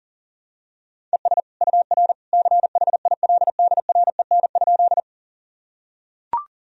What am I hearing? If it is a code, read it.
Morse code, heard as a keyed tone, about 40 wpm, 700 Hz. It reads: ES UR CHILDREN?
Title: Sentences from Top 300 Words and Abbreviations 40wpm